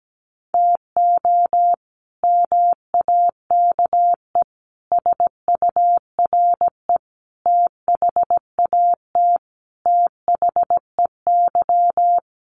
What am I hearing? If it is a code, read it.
Morse code, heard as a keyed tone, about 17 wpm, 700 Hz. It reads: TO MAKE SURE THAT THEY
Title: Sets of 5 Words Encore2 17wpm